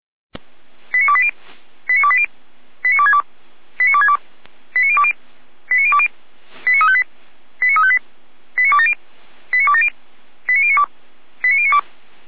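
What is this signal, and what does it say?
Five-tone selective call (ZVEI), heard as quick runs of five tones, tones of about 70 ms. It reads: 89280 89280 89262 89262 89020 89020 89358 89358 89270 89270 89092 89092
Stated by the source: ZVEI five-tone selective call 89330 received on SDR.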